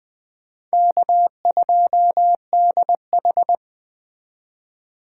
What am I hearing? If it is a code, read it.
Morse code, heard as a keyed tone, about 20 wpm, 700 Hz. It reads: K2DH